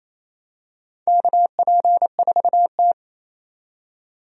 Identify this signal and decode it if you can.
Morse code, heard as a keyed tone, about 28 wpm, 700 Hz. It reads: KP4T
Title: DXCC-100 Call Signs - 4 Characters - 28wpm